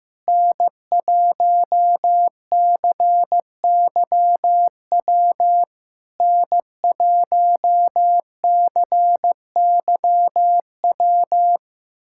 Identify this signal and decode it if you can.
Morse code, heard as a keyed tone, about 15 wpm, 700 Hz. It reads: N1CYW N1CYW
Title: SOTA QSO Practice 15wpm